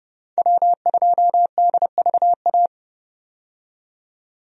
Morse code, heard as a keyed tone, about 31 wpm, 700 Hz. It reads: W2DVA
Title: USA Call Signs - 5 Characters - 30wpm